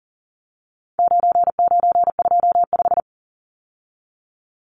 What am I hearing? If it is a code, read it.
Morse code, heard as a keyed tone, about 40 wpm, 700 Hz. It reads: 9925
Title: Numbers - 4-Digit 40wpm